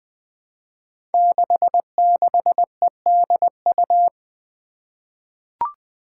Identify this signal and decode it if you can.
Morse code, heard as a keyed tone, about 20 wpm, 700 Hz. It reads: 66EDU
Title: Five Challenging Letter-Number Combinations 20wpm